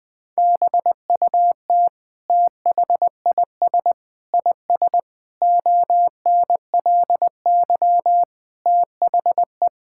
Morse code, heard as a keyed tone, about 20 wpm, 700 Hz. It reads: BUT THIS IS ONLY THE
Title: Sentences from Top 1000 Words - Review Part 5 - 20wpm